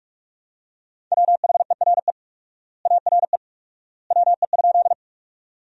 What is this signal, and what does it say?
Morse code, heard as a keyed tone, about 46 wpm, 700 Hz. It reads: WHERE ARE WE?
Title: 4x Word-Spacing - Sentences from Top 200 Words - 45wpm